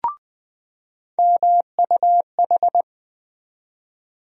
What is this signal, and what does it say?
Morse code, heard as a keyed tone, about 20 wpm, 700 Hz. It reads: MUH
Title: USA Call Sign Postfixes - 3 Letters 20wpm